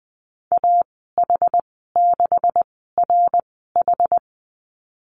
Morse code, heard as a keyed tone, about 20 wpm, spaced as for 14 wpm, 700 Hz. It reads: AH6RH
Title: 186 Lesson 30g - Letter b - Call Signs 20wpm